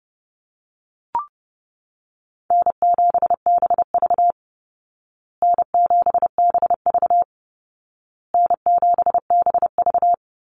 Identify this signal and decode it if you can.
Morse code, heard as a keyed tone, about 28 wpm, 700 Hz. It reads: N7BV N7BV N7BV
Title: Call Signs Repeated 2x - 30wpm